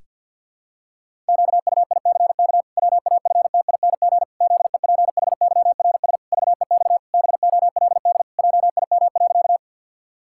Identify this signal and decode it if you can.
Morse code, heard as a keyed tone, about 50 wpm, 700 Hz. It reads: QUICK WAFTING ZEPHYRS VEX BOLD JIM.
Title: Pangram Challenge v2 - 50wpm